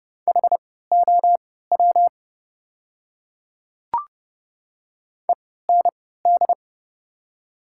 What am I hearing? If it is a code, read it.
Morse code, heard as a keyed tone, about 30 wpm, spaced as for 17 wpm, 700 Hz. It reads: HOW END
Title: 3x Character Spacing - Top 200 Words - 30wpm